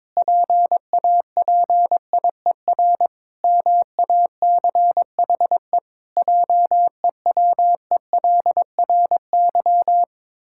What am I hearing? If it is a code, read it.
Morse code, heard as a keyed tone, about 22 wpm, 700 Hz. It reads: PAPIER MACHE JEWELRY
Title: Pangram Challenge v2 - 22wpm